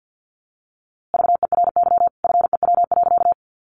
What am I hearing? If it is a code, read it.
Morse code, heard as a keyed tone, about 50 wpm, 700 Hz. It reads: VERY FEW.
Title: A Sound Of Thunder 50wpm